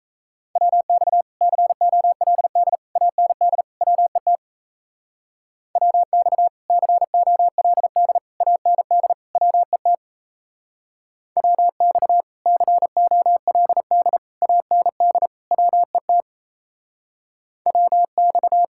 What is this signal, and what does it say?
Morse code, heard as a keyed tone, about 36 wpm, 700 Hz. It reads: WX COLD AND WET WX COLD AND WET WX COLD AND WET WX